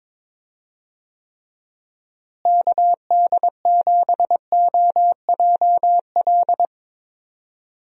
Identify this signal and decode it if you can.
Morse code, heard as a keyed tone, about 22 wpm, 700 Hz. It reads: KD7OJL
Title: USA Call Signs - 6 Characters - Repeated 2x - 22wpm